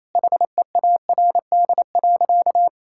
Morse code, heard as a keyed tone, about 28 wpm, 700 Hz. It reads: HEARD.
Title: Sentences from Top 1600 Words 28wpm